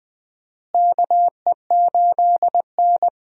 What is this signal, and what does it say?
Morse code, heard as a keyed tone, about 21 wpm, 700 Hz. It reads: KE8N
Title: USA Call Signs - 4 Characters - Repeated 3x - 20wpm